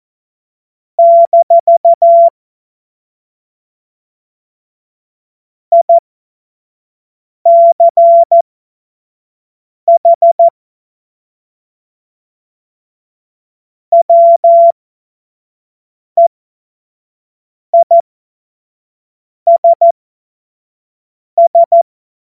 Morse code, heard as a keyed tone, about 14 wpm, spaced as for 5 wpm, 680 Hz. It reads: - ICH WEISS
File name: Morsekurs / Witze / Witz 03. Mai 2022 25 F70